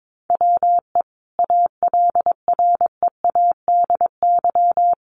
Morse code, heard as a keyed tone, about 22 wpm, 700 Hz. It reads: WE ALREADY
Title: Sentences from Top 700 Words 22wpm